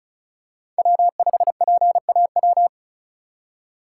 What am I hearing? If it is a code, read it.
Morse code, heard as a keyed tone, about 36 wpm, 700 Hz. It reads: W5PAW
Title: USA Call Signs - 5 Characters - 3x Character Spacing - 35wpm